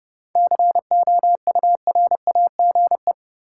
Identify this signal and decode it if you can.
Morse code, heard as a keyed tone, about 30 wpm, 700 Hz. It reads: COURAGE
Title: Seven Letter Words - 30wpm